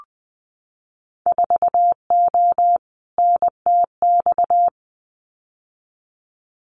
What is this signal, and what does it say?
Morse code, heard as a keyed tone, about 20 wpm, 700 Hz. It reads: 4O NTX
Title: Winter Field Day Exchanges 20wpm